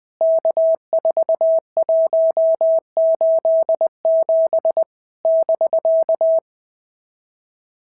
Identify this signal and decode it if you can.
Morse code, heard as a keyed tone, about 20 wpm, 650 Hz. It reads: K4187 <BK>
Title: POTA QSO Practice - 20wpm